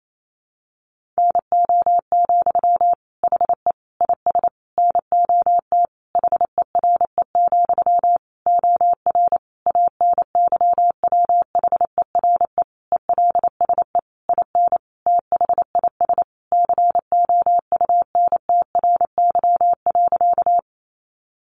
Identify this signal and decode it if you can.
Morse code, heard as a keyed tone, about 28 wpm, 700 Hz. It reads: NO, HE IS NOT HERE, OR ANYWHERE ELSE IN THIS COUNTRY.
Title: Sentences from Top 1400 Words 28wpm